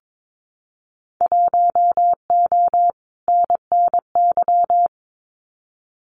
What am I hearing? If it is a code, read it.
Morse code, heard as a keyed tone, about 22 wpm, 700 Hz. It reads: 1O NNY
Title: Winter Field Day Exchanges Encore 22wpm